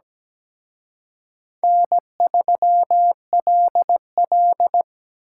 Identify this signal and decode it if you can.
Morse code, heard as a keyed tone, about 20 wpm, 700 Hz. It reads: N3LL
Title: Speed Racing - USA 1x2 Call Signs 17wpm